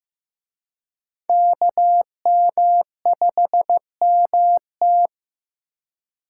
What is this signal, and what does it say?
Morse code, heard as a keyed tone, about 18 wpm, 700 Hz. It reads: KM5MT
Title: Speed Racing - USA Call Signs 15wpm